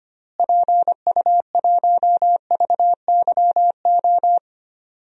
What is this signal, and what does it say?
Morse code, heard as a keyed tone, about 25 wpm, 700 Hz. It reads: PU1VYO
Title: DXCC-100 Call Signs - 6 Characters - Repeated 2x - 25wpm